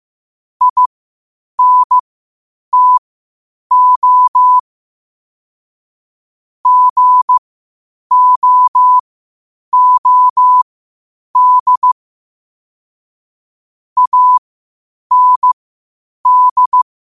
Morse code, heard as a keyed tone, about 15 wpm, spaced as for 8 wpm, 1000 Hz. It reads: INTO GOOD AND
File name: qotdfiles / 8wpm